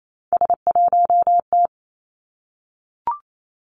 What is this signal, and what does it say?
Morse code, heard as a keyed tone, about 28 wpm, 700 Hz. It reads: S1T
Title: Three Challenging Letter-Number Combinations 28wpm